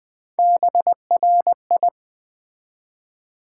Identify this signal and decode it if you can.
Morse code, heard as a keyed tone, about 20 wpm, 700 Hz. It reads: BRI
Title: Build A Word - 3 Letters At A Time - 6-Letter Words - 20wpm